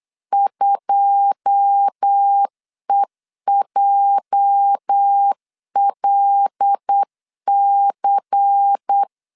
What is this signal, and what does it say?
Morse code, heard as a keyed tone, about 8 wpm, 800 Hz. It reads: 2EJLC
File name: LEZIONI / 9 2867 8wpm 3440